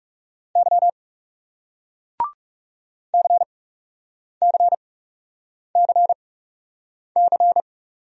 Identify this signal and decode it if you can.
Morse code, heard as a keyed tone, about 40 wpm, spaced as for 8 wpm, 700 Hz. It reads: Y CCCC